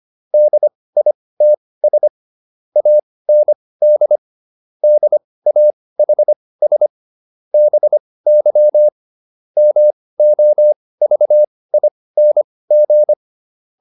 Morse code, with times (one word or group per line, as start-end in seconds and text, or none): 0.3-2.1 DITS
2.8-4.2 AND
4.8-6.9 DAHS
7.5-8.9 BY
9.6-13.1 MOVING